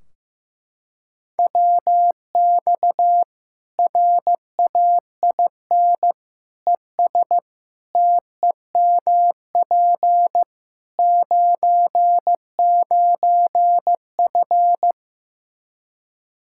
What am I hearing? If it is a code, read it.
Morse code, heard as a keyed tone, about 15 wpm, 700 Hz. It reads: WX RAIN ES TEMP 99F
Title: Practice QSO - Exchange 2 - Info - 15wpm